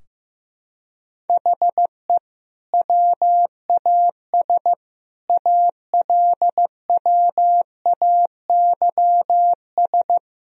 Morse code, heard as a keyed tone, about 15 wpm, 700 Hz. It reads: HE WAS ALWAYS
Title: Sentences from Top 1000 Words - Review Part 4 - 15wpm